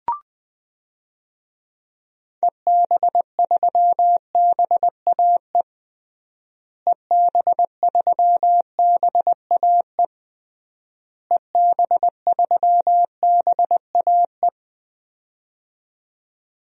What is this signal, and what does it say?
Morse code, heard as a keyed tone, about 20 wpm, 700 Hz. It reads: EB3BAE EB3BAE EB3BAE
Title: DXCC-100 Call Signs - 6 Characters - Repeated 3x - 20wpm